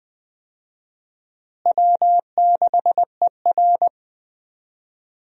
Morse code, heard as a keyed tone, about 20 wpm, 700 Hz. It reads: W6ER
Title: USA Call Signs - 4 Characters - Repeated 2x - 20wpm